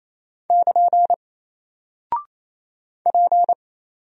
Morse code, heard as a keyed tone, about 27 wpm, 700 Hz. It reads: <KN> P
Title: Single Character (Letter:Number:Punctuation:Prosign) 28wpm